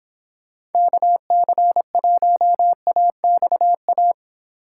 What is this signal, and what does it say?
Morse code, heard as a keyed tone, about 29 wpm, 700 Hz. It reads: KC1AXA